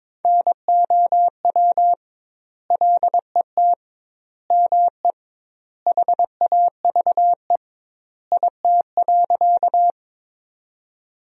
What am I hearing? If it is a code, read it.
Morse code, heard as a keyed tone, about 22 wpm, 700 Hz. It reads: NOW LET ME HAVE IT.
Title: 2x Word-Spacing - Sentences from Top 300 Words - 22wpm